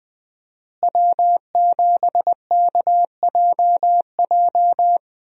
Morse code, heard as a keyed tone, about 20 wpm, 700 Hz. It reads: W7KJJ
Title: USA Call Signs - 5 Characters - Repeated 2x - 20wpm